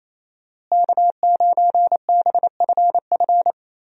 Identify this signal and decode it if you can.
Morse code, heard as a keyed tone, about 28 wpm, 700 Hz. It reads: K9BFF
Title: USA Call Signs - 5 Characters - Repeated 2x - 28wpm